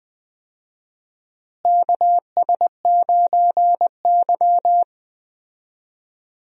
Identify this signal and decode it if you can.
Morse code, heard as a keyed tone, about 21 wpm, 700 Hz. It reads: KS9Y
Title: USA Call Signs - 4 Characters - Repeated 3x - 20wpm